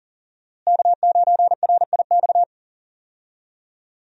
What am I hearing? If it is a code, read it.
Morse code, heard as a keyed tone, about 40 wpm, 700 Hz. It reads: K9RIX
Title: USA Call Signs - 5 Characters - Repeated 2x - 40wpm